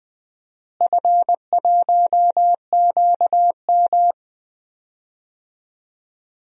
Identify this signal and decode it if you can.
Morse code, heard as a keyed tone, about 20 wpm, 700 Hz. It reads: F1QM
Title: DXCC-100 Call Signs - 4 Characters - Repeated 2x - 20wpm